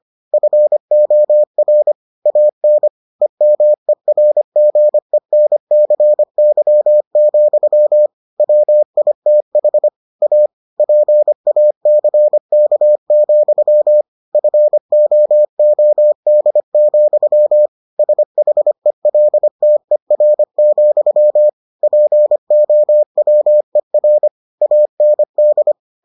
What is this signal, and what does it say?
Morse code, heard as a keyed tone, about 25 wpm, 600 Hz. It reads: FOR AN EMERGENCY, WITH A PACK, FOOD, SHELTER, POWER AND